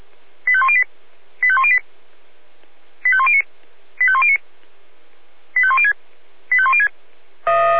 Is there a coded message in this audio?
Yes: five-tone selective call (ZVEI) from 0.5 to 6.9 s, heard as quick runs of five tones, tones of about 70 ms. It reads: 85108 85108 85109 85109 85106 85106